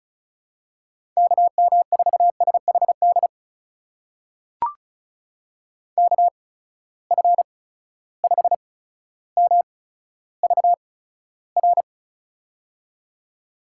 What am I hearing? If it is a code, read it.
Morse code, heard as a keyed tone, about 36 wpm, spaced as for 15 wpm, 700 Hz. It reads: KM4SHD KF5MVR
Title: USA Call Signs - 6 Characters - 8x Character Spacing - 35wpm